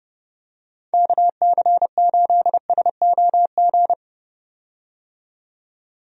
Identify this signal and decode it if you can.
Morse code, heard as a keyed tone, about 32 wpm, 700 Hz. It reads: KC8SOG
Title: USA Call Signs - 6 Characters - 30wpm